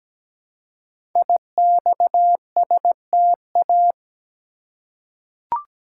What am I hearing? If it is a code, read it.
Morse code, heard as a keyed tone, about 17 wpm, 700 Hz. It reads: IXSTA
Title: Code Groups - 5 Letters - 17wpm